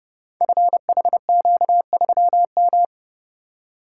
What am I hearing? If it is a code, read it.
Morse code, heard as a keyed tone, about 30 wpm, 700 Hz. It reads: FHQ3M
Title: Code Groups - 5 Letter:Numbers - 30wpm